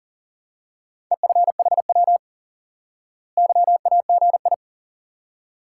Morse code, heard as a keyed tone, about 40 wpm, 700 Hz. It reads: EFHW YAGI
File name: IQE Course 3 - Lesson 89 - YAGI - Two Elements - 40wpm